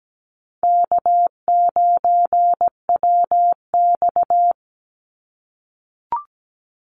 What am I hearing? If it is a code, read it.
Morse code, heard as a keyed tone, about 17 wpm, 700 Hz. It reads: K9WX
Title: Easy Call Signs - 17wpm